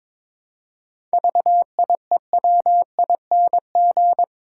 Morse code, heard as a keyed tone, about 22 wpm, 700 Hz. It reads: VIEWING
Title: Pangram Challenge v2 - 22wpm